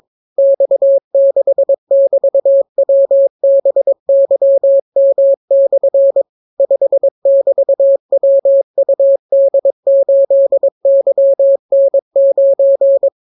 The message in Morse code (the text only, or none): X6<BT>WBYM/ 5<BT>WUD8YN9